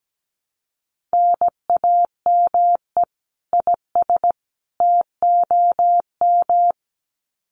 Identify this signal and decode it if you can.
Morse code, heard as a keyed tone, about 17 wpm, 700 Hz. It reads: NAME IS TOM
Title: Common QSO Phrases 17wpm